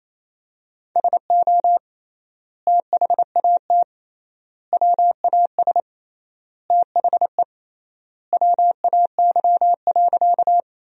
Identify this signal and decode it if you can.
Morse code, heard as a keyed tone, about 27 wpm, 700 Hz. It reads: SO THAT WAS THE WAY.
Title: 3x Word-Spacing - Sentences from Top 100 Words - 28wpm